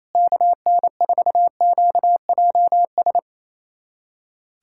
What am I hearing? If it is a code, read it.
Morse code, heard as a keyed tone, about 28 wpm, 700 Hz. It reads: KN4QJS
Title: USA Callsign DE Callsign 28wpm